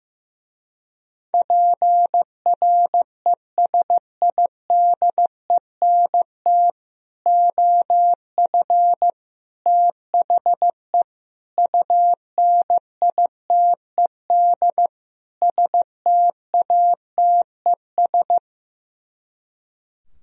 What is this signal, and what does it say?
Morse code, heard as a keyed tone, about 15 wpm, 700 Hz. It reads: PRESIDENT OF THE UNITED STATES
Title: Speed-Racing - Sets of 5 Words 15wpm